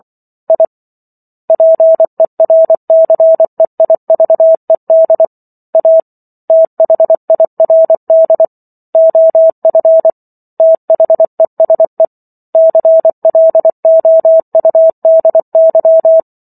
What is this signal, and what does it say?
Morse code, heard as a keyed tone, about 24 wpm, 650 Hz. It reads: I PERCEIVED A THIRD OF THESE CLOUDY